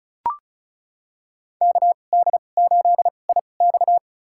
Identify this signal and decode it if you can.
Morse code, heard as a keyed tone, about 34 wpm, spaced as for 25 wpm, 700 Hz. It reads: KD8IX
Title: USA Call Signs - 5 Characters - 2x Character Spacing - 35wpm